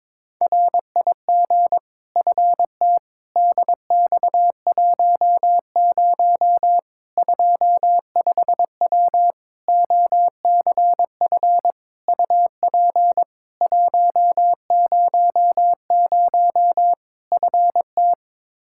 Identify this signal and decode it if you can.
Morse code, heard as a keyed tone, about 21 wpm, 700 Hz. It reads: RIG FT DX10 25W OCF UP 100 FT
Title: Practice QSO - Exchange 2 - Info - 22wpm